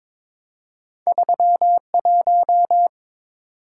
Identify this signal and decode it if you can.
Morse code, heard as a keyed tone, about 23 wpm, 700 Hz. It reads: 31
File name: practice / 22wpm / Two Letter-Number Combinations - 22wpm